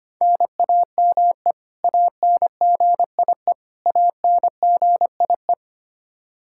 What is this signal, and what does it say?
Morse code, heard as a keyed tone, about 25 wpm, 700 Hz. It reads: NAME ANGIE ANGIE
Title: Practice QSO - Exchange 1 - Beginning - 25wpm